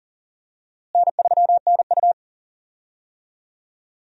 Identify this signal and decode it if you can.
Morse code, heard as a keyed tone, about 40 wpm, 700 Hz. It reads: N3NU
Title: USA Call Signs - 4 Characters - Repeated 3x - 40wpm